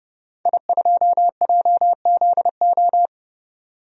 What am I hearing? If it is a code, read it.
Morse code, heard as a keyed tone, about 30 wpm, 700 Hz. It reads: I2JZO
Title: DXCC-100 Call Signs - 5 Characters - 30wpm